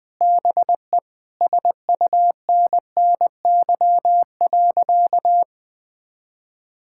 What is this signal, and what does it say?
Morse code, heard as a keyed tone, about 20 wpm, 700 Hz. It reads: BE SUNNY.
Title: Sentences with 3-Word Phrases - Spoken Beginning - 20wpm